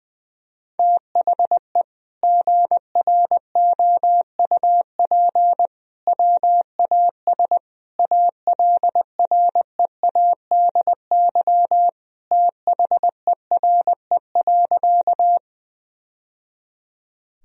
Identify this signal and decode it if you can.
Morse code, heard as a keyed tone, about 21 wpm, 700 Hz. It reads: THE GROUP WAS ALREADY THERE.
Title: Sentences from Top 1000 Words - Review Part 1 - 20wpm